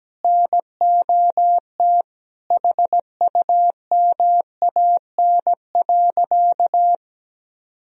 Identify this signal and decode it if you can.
Morse code, heard as a keyed tone, about 17 wpm, 700 Hz. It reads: NOT HUMAN.
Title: Sentences from Top 1000 Words - Review Part 3 - 17wpm